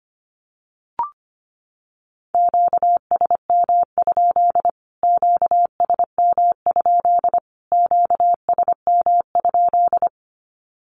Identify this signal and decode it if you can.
Morse code, heard as a keyed tone, about 25 wpm, 700 Hz. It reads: QSM? QSM? QSM?
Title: Q-Codes Repeated 3x 25wpm